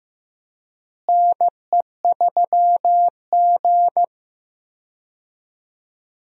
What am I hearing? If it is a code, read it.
Morse code, heard as a keyed tone, about 15 wpm, 700 Hz. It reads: NE3G
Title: USA Call Signs - 4 Characters - Repeated 3x - 15wpm